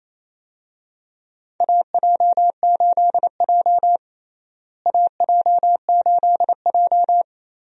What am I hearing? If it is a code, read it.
Morse code, heard as a keyed tone, about 28 wpm, 700 Hz. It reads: AJ8J AJ8J
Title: USA Call Signs - 4 Characters - Repeated 3x - 28wpm